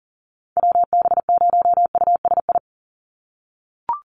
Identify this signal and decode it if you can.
Morse code, heard as a keyed tone, about 40 wpm, 700 Hz. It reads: WB0USI
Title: NA SOTA Activator Callsigns 40wpm